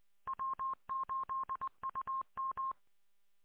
Morse code, heard as a keyed tone, about 20 wpm, 1070 Hz. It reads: W8UM